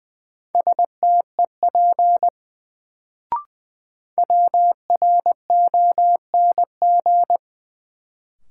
Morse code, heard as a keyed tone, about 20 wpm, 700 Hz. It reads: STEP WRONG
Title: Sets of 5 Words Encore - New Word Drill - 20wpm